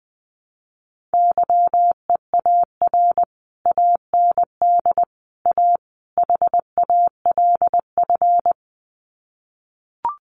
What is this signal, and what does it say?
Morse code, heard as a keyed tone, about 20 wpm, 700 Hz. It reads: YEAR AND A HALF
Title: Sets of 4 Words 20wpm